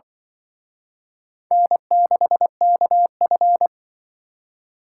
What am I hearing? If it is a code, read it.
Morse code, heard as a keyed tone, about 24 wpm, 700 Hz. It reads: N6KF
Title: Speed Racing - USA 1x2 Call Signs 20wpm